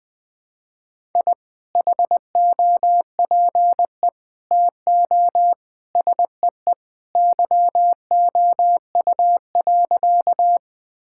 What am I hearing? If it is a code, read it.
Morse code, heard as a keyed tone, about 20 wpm, 700 Hz. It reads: I HOPE TO SEE YOU.
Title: Sentences with 2-Word Phrases - 20wpm